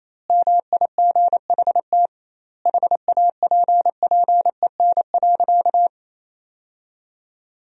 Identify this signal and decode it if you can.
Morse code, heard as a keyed tone, about 28 wpm, 700 Hz. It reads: MIGHT HAPPEN.
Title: 2x Word-Spacing - Sentences from Top 400 Words - 28wpm